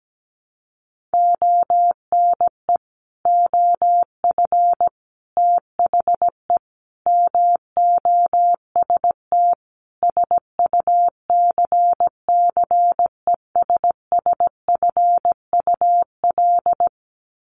Morse code, heard as a keyed tone, about 17 wpm, 700 Hz. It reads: ONE OF THE MOST SUCCESSFUL
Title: Sets of 5 Words Encore2 17wpm